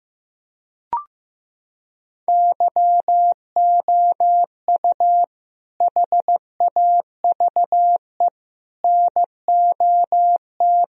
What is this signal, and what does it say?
Morse code, heard as a keyed tone, about 15 wpm, 700 Hz. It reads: YOU HAVE NOT
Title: Sentences from Top 1000 Words - Review Part 2 - 15wpm